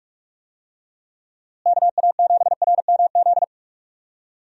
Morse code, heard as a keyed tone, about 46 wpm, 700 Hz. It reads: KA7RMZ
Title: USA Call Signs - 6 Characters - Repeated 2x - 45wpm